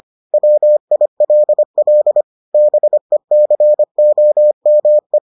Morse code, heard as a keyed tone, about 25 wpm, 600 Hz. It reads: WILL BECOME